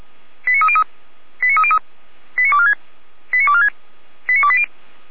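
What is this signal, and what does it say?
Five-tone selective call (ZVEI), heard as quick runs of five tones, tones of about 70 ms. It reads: 89303 89303 89246 89246 89280
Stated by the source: ZVEI five-tone call received by SDR, from an archive run by Feuerwehr Orlamünde.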